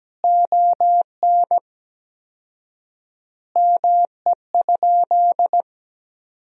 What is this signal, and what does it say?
Morse code, heard as a keyed tone, about 17 wpm, 700 Hz. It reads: ON ME?
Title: 4x Word-Spacing - Sentences from Top 500 Words - 17wpm